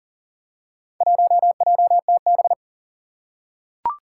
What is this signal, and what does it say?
Morse code, heard as a keyed tone, about 40 wpm, 700 Hz. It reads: 1JTB
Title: Four Challenging Letter-Number Combinations 40wpm